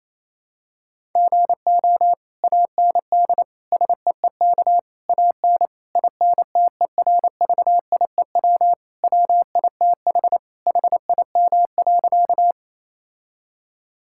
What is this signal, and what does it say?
Morse code, heard as a keyed tone, about 28 wpm, 700 Hz. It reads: GO AND SEEK AN INTERVIEW WITH HIM.
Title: Sentences from Top 1600 Words 28wpm